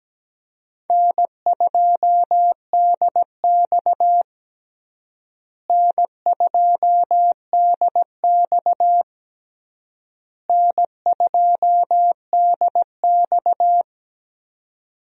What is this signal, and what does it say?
Morse code, heard as a keyed tone, about 17 wpm, 700 Hz. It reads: N2DX N2DX N2DX
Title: USA Call Signs - 4 Characters - Repeated 3x - 17wpm